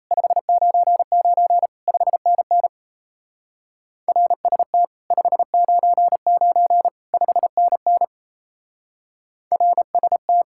Morse code, heard as a keyed tone, about 37 wpm, 700 Hz. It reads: 599 5NN RST 599 5NN RST